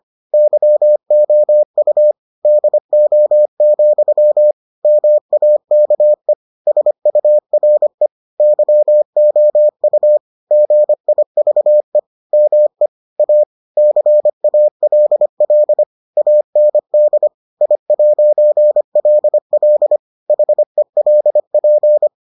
Morse code, heard as a keyed tone, about 25 wpm, 600 Hz. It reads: YOU DO, MAKE SURE YOU GIVE ME A CALL AND I'LL HELP